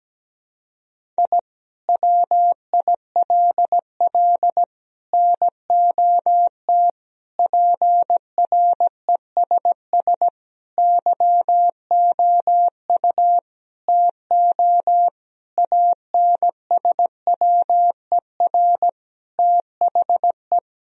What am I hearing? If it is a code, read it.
Morse code, heard as a keyed tone, about 17 wpm, 700 Hz. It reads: I WILL NOT PRESS YOU TO ANSWER THE
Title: Sentences from Top 1000 Words - Review Part 1 - 17wpm